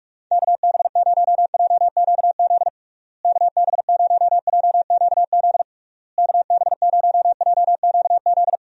Morse code, heard as a keyed tone, about 46 wpm, 700 Hz. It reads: KB0JQZ KB0JQZ KB0JQZ
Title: USA Call Signs - 6 Characters - Repeated 3x - 45wpm